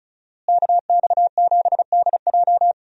Morse code, heard as a keyed tone, about 36 wpm, 700 Hz. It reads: KX7DJ
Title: USA Call Signs - 5 Characters - Repeated 3x - 35wpm